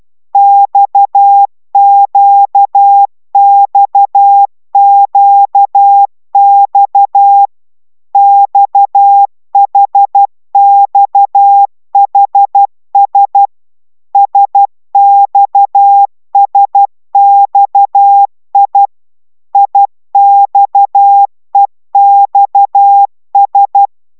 Morse code, heard as a keyed tone, about 12 wpm, 800 Hz. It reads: XQXQX XHXHS SXSXI IXEXS
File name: taviro tanulas / cw 69 16 10